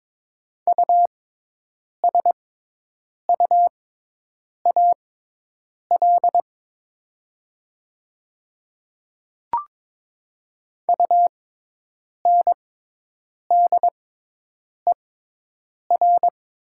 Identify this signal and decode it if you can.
Morse code, heard as a keyed tone, about 21 wpm, spaced as for 8 wpm, 700 Hz. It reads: USUAL UNDER
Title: 6x Character Spacing - Top 500 Words - 22wpm